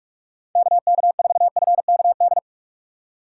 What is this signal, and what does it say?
Morse code, heard as a keyed tone, about 46 wpm, 700 Hz. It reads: KK4FKD
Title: USA Call Signs - 2x Character Spacing - 45wpm